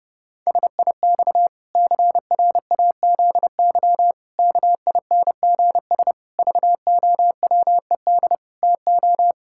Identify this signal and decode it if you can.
Morse code, heard as a keyed tone, about 30 wpm, 700 Hz. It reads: SIX CRAZY KINGS VOWED TO
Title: Pangram Challenge v2 - 30wpm